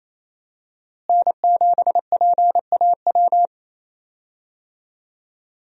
Morse code, heard as a keyed tone, about 28 wpm, 700 Hz. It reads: N7PAW
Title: USA Call Signs - 5 Characters - Repeated 3x - 28wpm